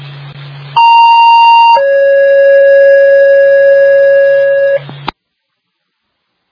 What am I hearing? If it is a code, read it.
Two-tone page, heard as a one-second tone, then a three-second tone, about 930 then 550 Hz. It is heard from 0.8 to 4.8 s.